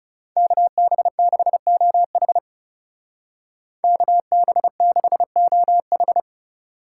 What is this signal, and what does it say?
Morse code, heard as a keyed tone, about 39 wpm, 700 Hz. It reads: KB6OH KB6OH